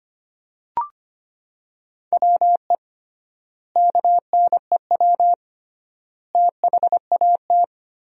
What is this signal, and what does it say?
Morse code, heard as a keyed tone, about 25 wpm, 700 Hz. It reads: WE KNEW THAT
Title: 3x Word-Spacing - Sentences from Top 400 Words - 25wpm